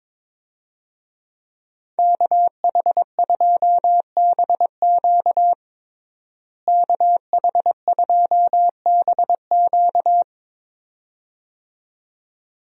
Morse code, heard as a keyed tone, about 22 wpm, 700 Hz. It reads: KH2BQ KH2BQ
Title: USA Call Signs - 5 Characters - Repeated 2x - 22wpm